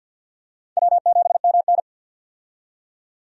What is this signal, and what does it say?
Morse code, heard as a keyed tone, about 51 wpm, 700 Hz. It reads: W7MN
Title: USA Call Signs - 4 Characters - 50wpm